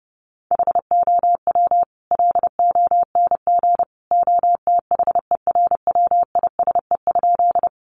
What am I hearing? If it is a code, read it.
Morse code, heard as a keyed tone, about 30 wpm, 700 Hz. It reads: HOW LONG OTHERWISE?
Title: Sentences from Top 1200 Words 30wpm